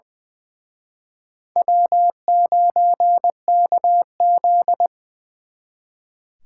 Morse code, heard as a keyed tone, about 25 wpm, 700 Hz. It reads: W9KZ
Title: Speed Racing - USA Call Signs 20wpm